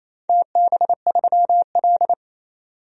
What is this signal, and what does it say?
Morse code, heard as a keyed tone, about 28 wpm, 700 Hz. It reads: TB3L
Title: DXCC-100 Call Signs - 4 Characters - 28wpm